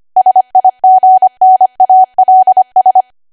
Morse code, heard as a keyed tone, about 25 wpm, 750 Hz. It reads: SIGNALS
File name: Archive / 25 WPM / 25WPM